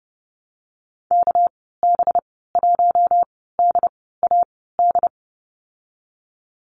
Morse code, heard as a keyed tone, about 30 wpm, spaced as for 17 wpm, 700 Hz. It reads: KB1DAD
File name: Lesson 37g - Letter x - Call Signs 30wpm